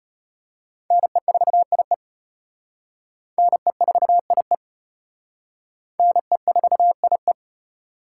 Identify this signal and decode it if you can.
Morse code, heard as a keyed tone, about 34 wpm, 700 Hz. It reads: NE4IE NE4IE NE4IE